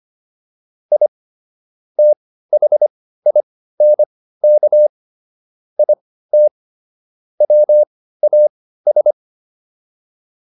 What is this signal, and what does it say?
Morse code, heard as a keyed tone, about 25 wpm, spaced as for 15 wpm, 600 Hz. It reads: I THINK IT WAS